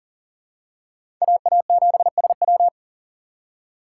Morse code, heard as a keyed tone, about 40 wpm, 690 Hz. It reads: AA7SW
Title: USA Call Signs - 4x Character Spacing - 40wpm